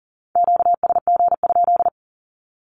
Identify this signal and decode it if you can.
Morse code, heard as a keyed tone, about 40 wpm, 700 Hz. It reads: QSG?
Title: Q-Codes 40wpm